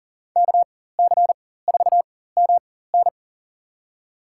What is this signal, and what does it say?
Morse code, heard as a keyed tone, about 37 wpm, spaced as for 18 wpm, 700 Hz. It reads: KC4MN
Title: USA Call Signs - 5 Characters - 4x Character Spacing - 40wpm